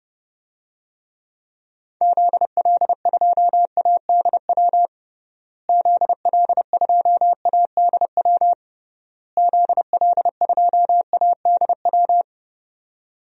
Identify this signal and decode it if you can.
Morse code, heard as a keyed tone, about 30 wpm, 700 Hz. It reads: ZL2ADW ZL2ADW ZL2ADW
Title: DXCC-100 Call Signs - 6 Characters - Repeated 3x - 30wpm